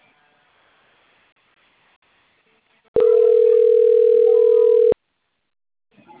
Ringback tone (North America), 440 + 480 Hz.